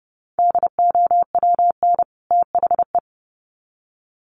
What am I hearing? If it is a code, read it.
Morse code, heard as a keyed tone, about 30 wpm, 700 Hz. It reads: DOWN THE
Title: Sets of 2 Words 30wpm